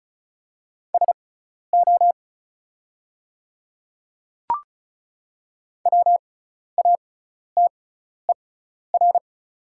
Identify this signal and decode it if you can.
Morse code, heard as a keyed tone, about 36 wpm, spaced as for 12 wpm, 700 Hz. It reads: SO WATER
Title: 6x Character Spacing - Top 100 Words - 35wpm